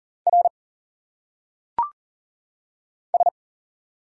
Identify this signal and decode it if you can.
Morse code, heard as a keyed tone, about 40 wpm, 700 Hz. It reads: R S
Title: Single Letters - Phonetic Pronunciation 40wpm